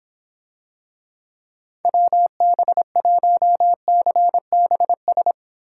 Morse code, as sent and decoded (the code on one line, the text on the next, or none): .-- -... .---- -.-. -... ...
WB1CBS